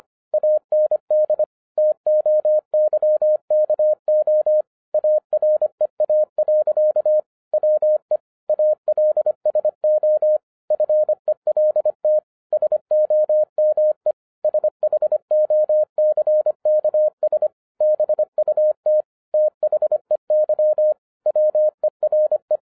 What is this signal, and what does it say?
Morse code, heard as a keyed tone, about 25 wpm, 610 Hz. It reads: AND TOYKO AREA. WE ALSO FELT SOME SHOCKS BUT THEY WERE
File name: 2011-07-mp3 / n34 THE 3.11 2 25wpm